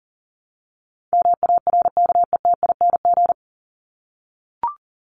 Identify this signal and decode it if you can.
Morse code, heard as a keyed tone, about 40 wpm, 700 Hz. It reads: MARKETING
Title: ING Suffix - Top 100 Words with Base Words 40wpm